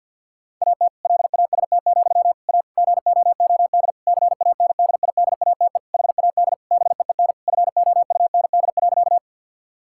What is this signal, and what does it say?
Morse code, heard as a keyed tone, about 50 wpm, 700 Hz. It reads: AT LAST, A GOOD CANDIDATE HAD BEEN FOUND.
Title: Sentences from Top 1600 Words 50wpm